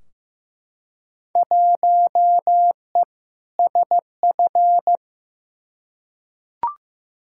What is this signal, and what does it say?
Morse code, heard as a keyed tone, about 15 wpm, 700 Hz. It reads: 1E SF
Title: Field Day Exchanges - 15wpm